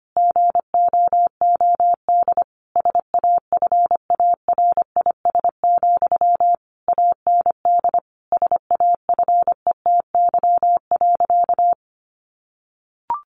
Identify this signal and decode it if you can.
Morse code, heard as a keyed tone, about 25 wpm, 700 Hz. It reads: GOOD SAFARIS, AND SAFETY.
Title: A Sound Of Thunder 25wpm